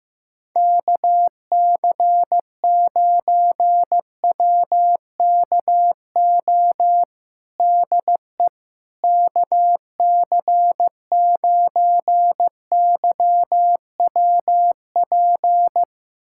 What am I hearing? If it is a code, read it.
Morse code, heard as a keyed tone, about 15 wpm, 700 Hz. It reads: KC9WKO DE KC9YWP
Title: USA Callsign DE Callsign 15wpm